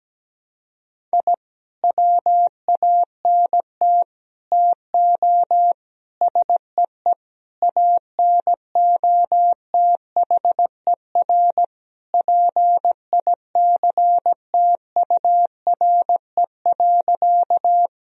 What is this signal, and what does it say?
Morse code, heard as a keyed tone, about 17 wpm, 700 Hz. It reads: I WANT TO SEE ANOTHER PICTURE.
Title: Sentences from Top 1000 Words - Review Part 1 - 17wpm